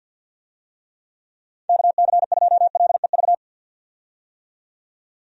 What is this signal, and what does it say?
Morse code, heard as a keyed tone, about 52 wpm, 700 Hz. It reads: KC2LEV
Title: USA Call Signs - 6 Characters - 50wpm